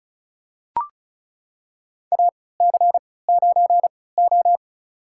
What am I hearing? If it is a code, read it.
Morse code, heard as a keyed tone, about 36 wpm, spaced as for 20 wpm, 690 Hz. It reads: AC9O
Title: USA Call Signs - 4 Characters - 3x Character Spacing - 35wpm